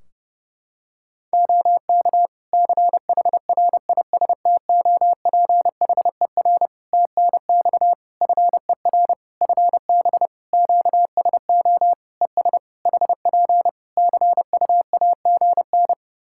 Morse code, heard as a keyed tone, about 30 wpm, 700 Hz. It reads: OK CHRISTOPHER TNX FER FB QSO ES HP CUAGN
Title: Practice QSO - Exchange 4 - Ending - 30wpm